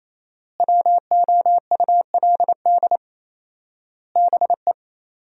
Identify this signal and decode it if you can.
Morse code, heard as a keyed tone, about 28 wpm, 700 Hz. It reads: WOULD BE
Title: 4x Word-Spacing - Sentences from Top 500 Words - 28wpm